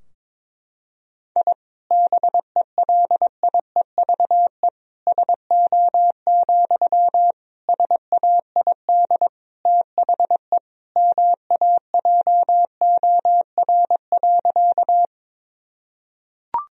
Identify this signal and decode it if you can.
Morse code, heard as a keyed tone, about 22 wpm, 700 Hz. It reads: I BELIEVE SO, SAID THE MAJOR.
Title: Sentences from Top 1000 Words - Review Part 4 - 22wpm